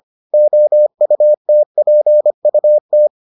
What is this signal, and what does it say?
Morse code, heard as a keyed tone, about 25 wpm, 600 Hz. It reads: OUTPUT